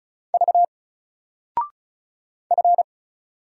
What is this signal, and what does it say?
Morse code, heard as a keyed tone, about 36 wpm, 700 Hz. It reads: V F
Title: Single Letter-Number 35wpm